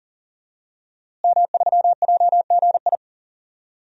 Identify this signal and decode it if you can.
Morse code, heard as a keyed tone, about 40 wpm, 700 Hz. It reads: M3JGI
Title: DXCC-100 Call Signs - 5 Characters - Repeated 2x - 40wpm